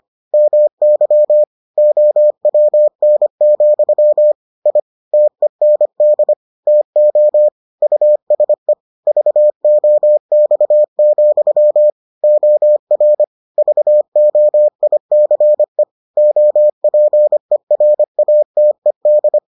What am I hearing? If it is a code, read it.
Morse code, heard as a keyed tone, about 25 wpm, 600 Hz. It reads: MY OWN, I TEND TO USE VOX, OR VOICE OPERATED